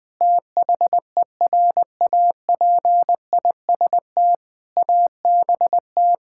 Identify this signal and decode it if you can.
Morse code, heard as a keyed tone, about 21 wpm, 700 Hz. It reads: THERAPIST ABT
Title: Practice QSO - Exchange 3 - More Info - 20wpm